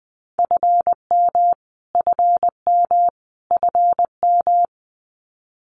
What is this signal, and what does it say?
Morse code, heard as a keyed tone, about 19 wpm, 700 Hz. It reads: FM FM FM
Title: Abbreviations Repeated 3x - 20wpm